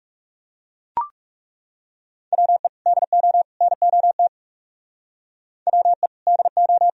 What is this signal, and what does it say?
Morse code, heard as a keyed tone, about 37 wpm, 700 Hz. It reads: WE DO NOT WE DO
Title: Speed-Racing - Sets of 3 Words 30wpm